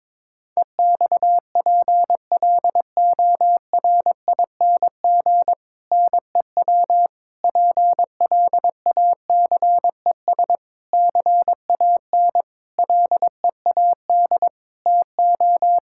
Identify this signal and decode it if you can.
Morse code, heard as a keyed tone, about 22 wpm, 700 Hz. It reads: EXPLORING NEW PLACES CAN LEAD TO
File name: Difficult Sentences with Hints - 22wpm